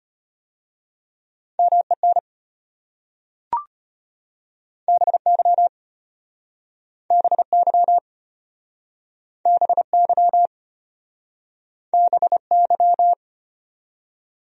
Morse code, heard as a keyed tone, about 32 wpm, 700 Hz. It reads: MEN BY BY BY BY